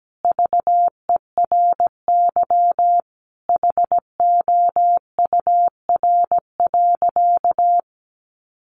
Morse code, heard as a keyed tone, about 17 wpm, 700 Hz. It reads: VERY HOUR.
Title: Sentences from Top 1500 Words 17wpm